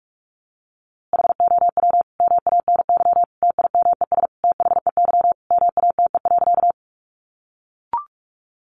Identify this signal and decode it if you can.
Morse code, heard as a keyed tone, about 44 wpm, 700 Hz. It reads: HOW MANY TIMES THEY MATE.
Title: A Sound Of Thunder 45wpm